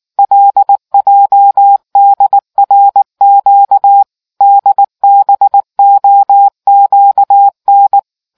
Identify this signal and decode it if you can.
Morse code, heard as a keyed tone, about 19 wpm, 800 Hz. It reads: LJDRQ DBOQN